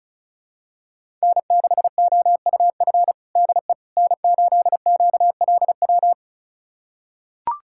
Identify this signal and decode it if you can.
Morse code, heard as a keyed tone, about 36 wpm, 700 Hz. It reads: N6OUF DE N8QLW
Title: USA Callsign DE Callsign - 5 Characters - 35wpm